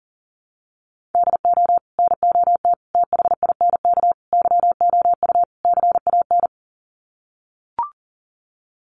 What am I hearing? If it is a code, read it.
Morse code, heard as a keyed tone, about 40 wpm, 700 Hz. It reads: DO NOT THINK YOU CAN
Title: Sets of 5 Words 40wpm